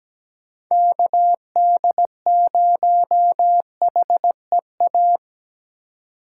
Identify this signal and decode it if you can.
Morse code, heard as a keyed tone, about 17 wpm, 700 Hz. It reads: KD0HEA
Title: USA Call Signs - 6 Characters - 3x Character Spacing - 17wpm